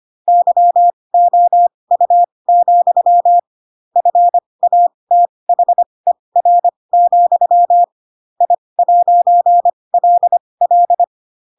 Morse code, heard as a keyed tone, about 25 wpm, spaced as for 20 wpm, 700 Hz. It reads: YOU, FATHER, I'LL
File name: Pinocchio / Pinocchio ch8 15 25 ik1hge